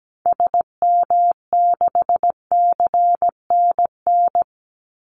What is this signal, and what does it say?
Morse code, heard as a keyed tone, about 17 wpm, 700 Hz. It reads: SM6CNN
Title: Call Signs - 17wpm